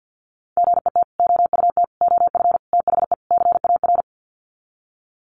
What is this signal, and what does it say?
Morse code, heard as a keyed tone, about 48 wpm, 700 Hz. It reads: GET OUT OF THE CAR
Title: Sets of 5 Words Encore2 50wpm